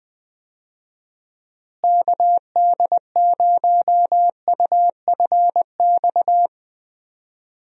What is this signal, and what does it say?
Morse code, heard as a keyed tone, about 20 wpm, 700 Hz. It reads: KD0UFX
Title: USA Call Signs - 6 Characters - Repeated 2x - 20wpm